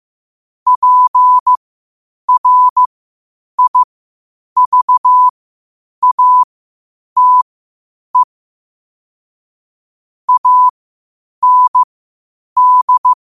Morse code, heard as a keyed tone, about 15 wpm, spaced as for 8 wpm, 1000 Hz. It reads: PRIVATE AND